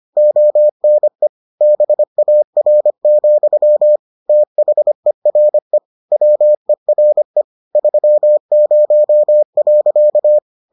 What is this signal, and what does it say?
Morse code, heard as a keyed tone, about 25 wpm, 600 Hz. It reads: ONE BAR, THERE WERE 30.